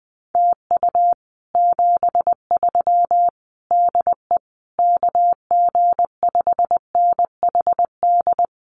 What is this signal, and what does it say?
Morse code, heard as a keyed tone, about 19 wpm, 700 Hz. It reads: TU 73 DE KG5NHD
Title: Common QSO Phrases Encore 20wpm